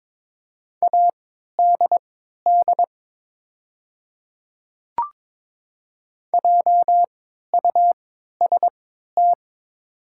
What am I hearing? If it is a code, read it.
Morse code, heard as a keyed tone, about 22 wpm, spaced as for 13 wpm, 700 Hz. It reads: ADD JUST
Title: 3x Character Spacing - Top 500 Words - 22wpm